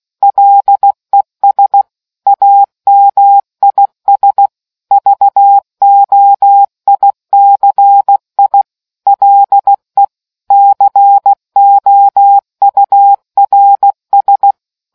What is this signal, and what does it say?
Morse code, heard as a keyed tone, about 16 wpm, 800 Hz. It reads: LES AMIS VOICI LE COURS